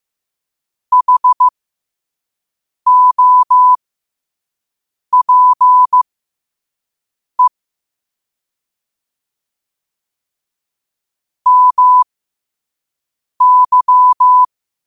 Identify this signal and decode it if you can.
Morse code, heard as a keyed tone, about 15 wpm, spaced as for 5 wpm, 1000 Hz. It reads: HOPE MY
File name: qotdfiles / 5wpm